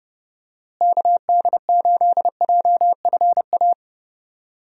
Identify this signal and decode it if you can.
Morse code, heard as a keyed tone, about 32 wpm, 700 Hz. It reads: KD8JFA
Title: USA Call Signs - 6 Characters - Repeated 3x - 30wpm